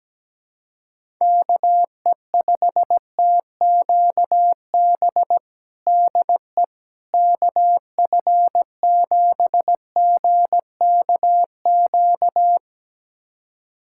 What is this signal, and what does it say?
Morse code, heard as a keyed tone, about 17 wpm, 700 Hz. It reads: KE5TQB DE KF7GKQ
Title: USA Callsign DE Callsign - 6 Characters - 17wpm